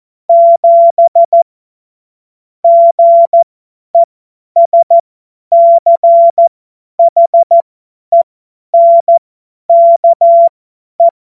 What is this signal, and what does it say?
Morse code, heard as a keyed tone, about 14 wpm, spaced as for 10 wpm, 680 Hz. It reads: 7 GESCHENKE
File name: Morsekurs / Witze / Witz 17. August 2021, 50 F70